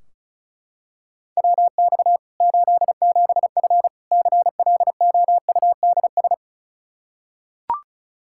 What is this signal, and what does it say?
Morse code, heard as a keyed tone, about 34 wpm, 700 Hz. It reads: WX 87F CLOUDS
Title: K1USN SST QSO Practice 35wpm